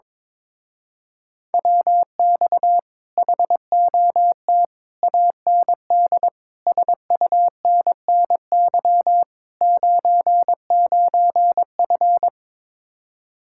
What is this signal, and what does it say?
Morse code, heard as a keyed tone, about 22 wpm, 700 Hz. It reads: WX HOT AND SUNNY 99F
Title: Speed-Racing Common QSO Phrases - 22wpm